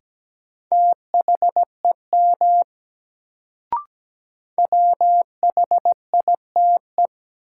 Morse code, heard as a keyed tone, about 17 wpm, 700 Hz. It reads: THEM WHITE
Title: Sets of 2 Words - New Word Drill - 17wpm